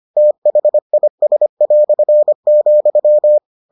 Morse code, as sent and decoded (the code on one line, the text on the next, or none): - .... .. ... .-..-. --..--
THIS",